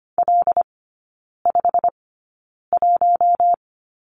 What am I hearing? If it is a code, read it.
Morse code, heard as a keyed tone, about 25 wpm, spaced as for 9 wpm, 700 Hz. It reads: L51
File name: Lesson 15e - Number 5 - Three Characters 25wpm